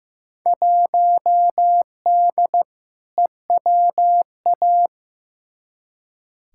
Morse code, heard as a keyed tone, about 15 wpm, 700 Hz. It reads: 1D EWA
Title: Field Day Exchanges - 15wpm